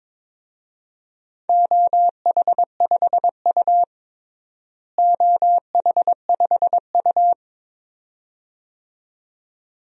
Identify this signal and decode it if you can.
Morse code, heard as a keyed tone, about 22 wpm, 700 Hz. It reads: OH5U OH5U
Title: DXCC-100 Call Signs - 4 Characters - Repeated 2x - 22wpm